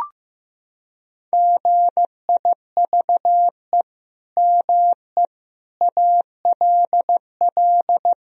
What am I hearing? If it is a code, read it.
Morse code, heard as a keyed tone, about 16 wpm, 700 Hz. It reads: GIVE ME ALL
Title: Sentences from Top 1000 Words - Review Part 3 - 15wpm